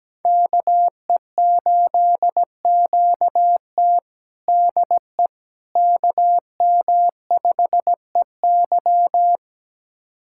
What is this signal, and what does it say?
Morse code, heard as a keyed tone, about 17 wpm, 700 Hz. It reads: KE8QT DE KM5EY
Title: USA Callsign DE Callsign - 5 Characters - 17wpm